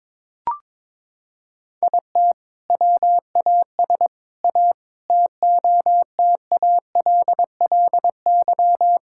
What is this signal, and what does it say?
Morse code, heard as a keyed tone, about 21 wpm, 700 Hz. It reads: IT WAS A TOTALLY
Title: Sentences from Top 1600 Words 22wpm